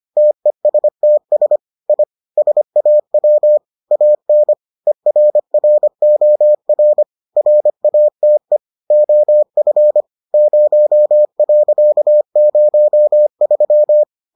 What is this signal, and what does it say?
Morse code, heard as a keyed tone, about 25 wpm, 600 Hz. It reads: TESTS I SAW AN ERROR RATE OF 0.03